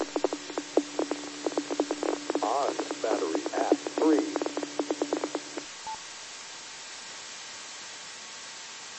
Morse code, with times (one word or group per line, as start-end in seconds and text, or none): none